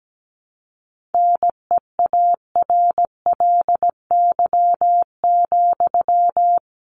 Morse code, heard as a keyed tone, about 17 wpm, 700 Hz. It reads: NEARLY,
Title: Sentences from Top 1300 Words 17wpm